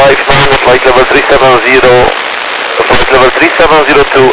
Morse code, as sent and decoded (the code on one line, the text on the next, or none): none